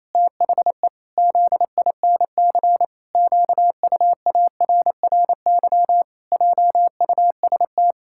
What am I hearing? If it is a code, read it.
Morse code, heard as a keyed tone, about 28 wpm, 700 Hz. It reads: THE ZINC QUARRY JUST
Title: Pangram Challenge v2 - 28wpm